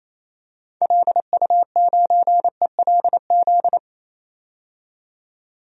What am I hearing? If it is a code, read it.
Morse code, heard as a keyed tone, about 28 wpm, 700 Hz. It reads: LU9ELZ
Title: DXCC-100 Call Signs - 6 Characters - 28wpm